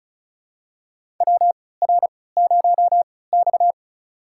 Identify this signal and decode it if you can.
Morse code, heard as a keyed tone, about 36 wpm, spaced as for 20 wpm, 690 Hz. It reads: WR0X
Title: USA Call Signs - 4 Characters - 3x Character Spacing - 35wpm